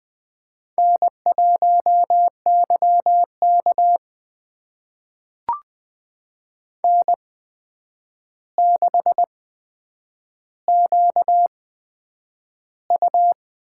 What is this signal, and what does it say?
Morse code, heard as a keyed tone, about 20 wpm, spaced as for 9 wpm, 700 Hz. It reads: N1YK N6QU
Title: USA Call Signs - 4 Characters - 8x Character Spacing - 20wpm